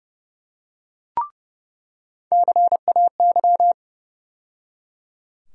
Morse code, heard as a keyed tone, about 32 wpm, 700 Hz. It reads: CAY
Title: USA Call Sign Postfixes - 3 Letters 30wpm